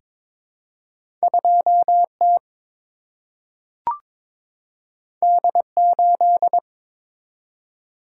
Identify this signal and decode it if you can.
Morse code, heard as a keyed tone, about 22 wpm, 700 Hz. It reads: 2T D8
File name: practice / 22wpm / Two Letter-Number Combinations - 22wpm